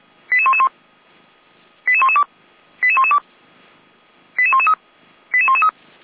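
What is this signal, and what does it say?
Five-tone selective call (ZVEI), heard as quick runs of five tones, tones of about 70 ms. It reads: 88111 88112 88112 88113 88113
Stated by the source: ZVEI five-tone selective call received by SDR, from feuerwehr-orlamuende.org fire brigade audio archive.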